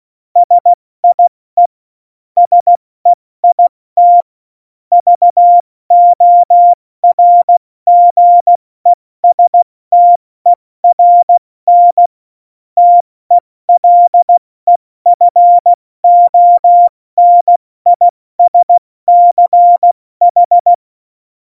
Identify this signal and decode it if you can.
Morse code, heard as a keyed tone, about 16 wpm, 700 Hz. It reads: SIE SEIT VORGESTERN TELEFONISCH
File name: Morsekurs / Witze / Witz 22. September 2020, 70 F80